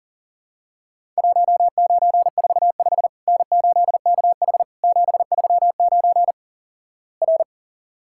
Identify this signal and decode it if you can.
Morse code, heard as a keyed tone, about 38 wpm, 700 Hz. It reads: 1945 N8KH 739 R
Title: Worked All Europe DX Contest - QTC Exchange - 40wpm